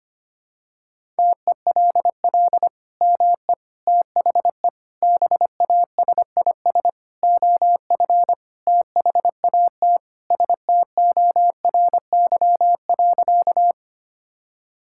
Morse code, heard as a keyed tone, about 25 wpm, 700 Hz. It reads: TELL ME THE BASIS OF THAT STORY.
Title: Sentences from Top 1000 Words - Review Part 5 - 25wpm